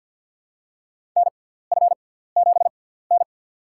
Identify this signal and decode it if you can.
Morse code, heard as a keyed tone, about 52 wpm, spaced as for 17 wpm, 700 Hz. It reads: NF7N
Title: USA Call Signs - 4 Characters - 6x Character Spacing - 50wpm